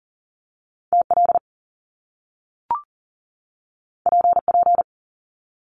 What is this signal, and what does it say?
Morse code, heard as a keyed tone, about 40 wpm, 700 Hz. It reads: TL PP
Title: Top Two Letter Combinations in Words 40wpm